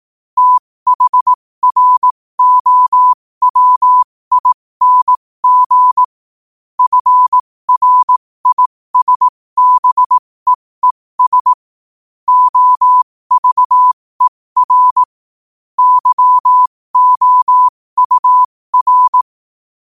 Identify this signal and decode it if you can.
Morse code, heard as a keyed tone, about 18 wpm, spaced as for 15 wpm, 1000 Hz. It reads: THROWING FRISBEES OVER YOUR
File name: qotdfiles / 15wpm